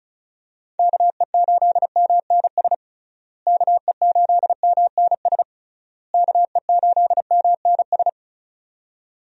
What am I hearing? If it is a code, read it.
Morse code, heard as a keyed tone, about 36 wpm, 700 Hz. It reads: KE8MNS KE8MNS KE8MNS
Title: USA Call Signs - 6 Characters - Repeated 3x - 35wpm